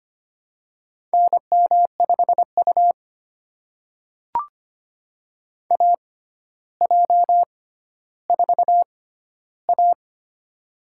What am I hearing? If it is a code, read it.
Morse code, heard as a keyed tone, about 25 wpm, spaced as for 13 wpm, 700 Hz. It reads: NM5U AJ4A
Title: USA Call Signs - 4 Characters - 6x Character Spacing - 25wpm